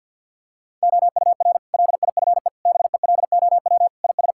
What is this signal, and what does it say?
Morse code, heard as a keyed tone, about 50 wpm, 700 Hz. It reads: OUR LIFE BELOW IS
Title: Sentences from Top 1000 Words - Review Part 5 - 50wpm